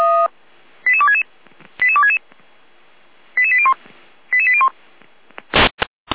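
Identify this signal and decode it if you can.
Five-tone selective call (ZVEI), heard as quick runs of five tones, tones of about 70 ms. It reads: 88266 88266 88801 88801
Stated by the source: ZVEI five-tone selective call received on SDR.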